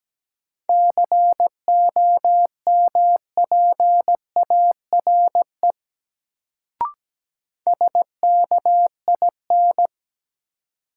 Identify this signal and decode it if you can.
Morse code, heard as a keyed tone, about 17 wpm, 700 Hz. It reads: COMPARE SKIN
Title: Sentences from Top 1300 Words - New Word Drill Alt - 17wpm